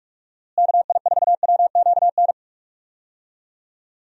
Morse code, heard as a keyed tone, about 42 wpm, 700 Hz. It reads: KI4WQN
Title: USA Call Signs - 6 Characters - Repeated 2x - 45wpm